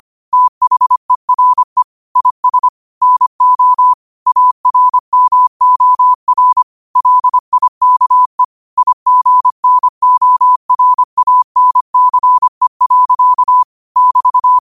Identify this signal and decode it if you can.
Morse code, heard as a keyed tone, about 25 wpm, 1000 Hz. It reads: THERE IS NO ARMOR LIKE IGNORANCE. <BT>